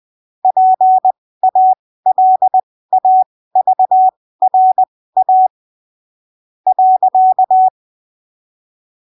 Morse code, heard as a keyed tone, about 20 wpm, spaced as for 15 wpm, 750 Hz. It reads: PALAVRA .
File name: arquivos / texto / texto 25 20ppm